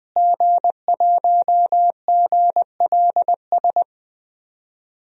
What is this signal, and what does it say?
Morse code, heard as a keyed tone, about 20 wpm, 700 Hz. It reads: G1GLS
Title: DXCC-100 Call Signs - 5 Characters - Repeated 2x - 20wpm